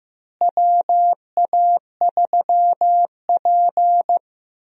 Morse code, heard as a keyed tone, about 15 wpm, 700 Hz. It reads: WA3P
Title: USA Call Signs - 4 Characters - Repeated 3x - 15wpm